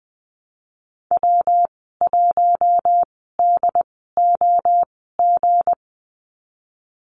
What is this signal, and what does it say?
Morse code, heard as a keyed tone, about 19 wpm, spaced as for 14 wpm, 700 Hz. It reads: W1DOG
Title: 179 Lesson 29g - Slash - Call Signs 20wpm